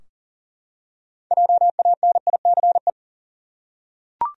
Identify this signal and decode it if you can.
Morse code, heard as a keyed tone, about 40 wpm, 700 Hz. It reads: JANICE
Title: USA Common Names - 40wpm